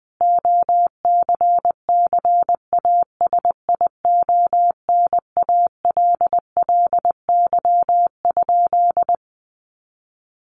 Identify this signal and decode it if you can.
Morse code, heard as a keyed tone, about 20 wpm, 700 Hz. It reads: OCCASIONALLY?
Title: Sentences from Top 1300 Words 20wpm